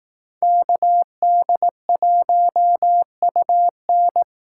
Morse code, heard as a keyed tone, about 18 wpm, 700 Hz. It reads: KD1UN
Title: Speed Racing - USA 2x2 Call Signs 15wpm